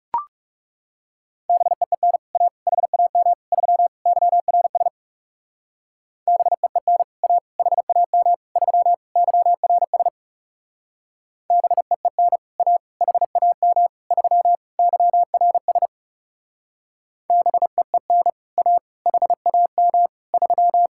Morse code, heard as a keyed tone, about 38 wpm, 700 Hz. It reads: BEEN A HAM 3 YRS BEEN A HAM 3 YRS BEEN A HAM 3 YRS BEEN A HAM 3